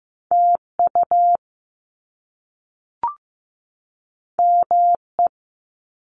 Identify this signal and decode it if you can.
Morse code, heard as a keyed tone, about 15 wpm, 700 Hz. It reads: TU ME
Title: Top Two Letter Combinations in Words 15wpm